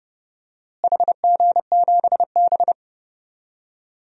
Morse code, heard as a keyed tone, about 30 wpm, 700 Hz. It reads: HG7B
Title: DXCC-100 Call Signs - 4 Characters - Repeated 2x - 30wpm